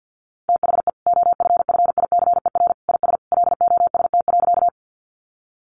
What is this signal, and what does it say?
Morse code, heard as a keyed tone, about 50 wpm, 700 Hz. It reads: THE OFFICER IS LOST.
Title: Sentences from Top 1400 Words 50wpm